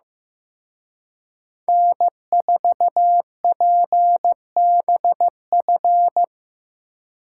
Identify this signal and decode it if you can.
Morse code, heard as a keyed tone, about 18 wpm, 700 Hz. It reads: N4PBF
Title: Speed Racing - USA 1x3 Call Signs 15wpm